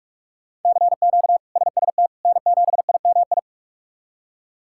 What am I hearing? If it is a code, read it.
Morse code, heard as a keyed tone, about 42 wpm, 700 Hz. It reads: CQ SST N7IMI
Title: K1USN SST QSO Practice - 45wpm